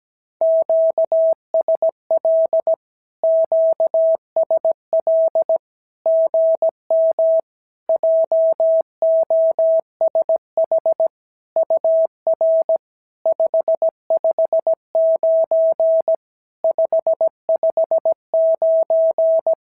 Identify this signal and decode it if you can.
Morse code, heard as a keyed tone, about 16 wpm, 650 Hz. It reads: QSL QSL GM JOSH UR 559 559
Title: SOTA QSO Practice 17wpm